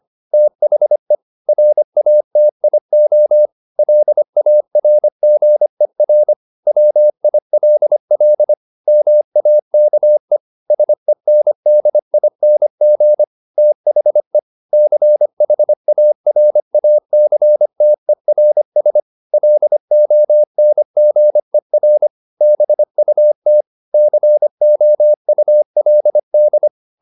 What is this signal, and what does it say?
Morse code, heard as a keyed tone, about 25 wpm, 600 Hz. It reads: THE RATIO LARGER WILL MAKE SENDING THE CHARACTERS LONGER BUT COULD